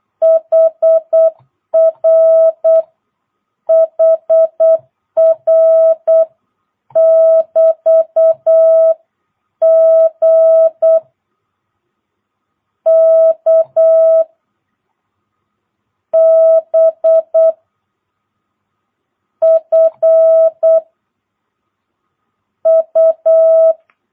Morse code, heard as a keyed tone, about 8 wpm, 650 Hz. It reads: HR HR <BT> G K B F U
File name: jh8cbh / 2022cw